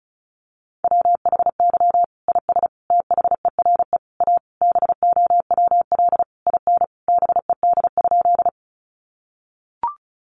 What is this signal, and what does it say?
Morse code, heard as a keyed tone, about 34 wpm, 700 Hz. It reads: WHY IS THERE A BOWL IN BED?
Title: Sentences from Top 1200 Words 35wpm